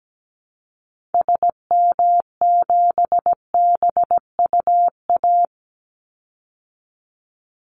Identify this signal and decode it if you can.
Morse code, heard as a keyed tone, about 17 wpm, 700 Hz. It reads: SM7BUA
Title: Call Signs Repeated 2x - 17wpm